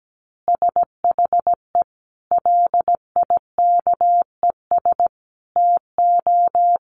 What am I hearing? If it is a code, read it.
Morse code, heard as a keyed tone, about 17 wpm, 700 Hz. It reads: SHE LIKES TO
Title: Additional Easy Phrases 17wpm